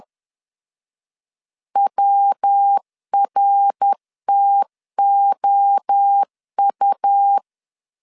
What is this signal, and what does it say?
Morse code, heard as a keyed tone, about 10 wpm, 800 Hz. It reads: WRTOU
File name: CW / DET / 1 3 1500C S C 10wpm 3335